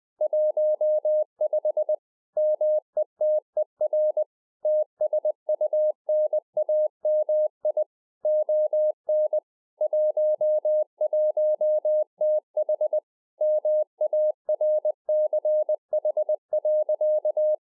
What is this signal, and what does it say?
Morse code, heard as a keyed tone, about 20 wpm, 610 Hz. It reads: 15 METER TSUNAMI ON 11TH MARCH.